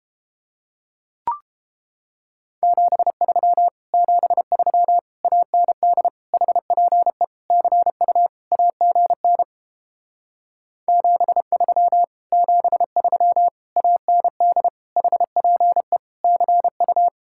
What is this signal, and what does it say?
Morse code, heard as a keyed tone, about 29 wpm, 700 Hz. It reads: 73 73 AND HPE CU AGN 73 73 AND HPE CU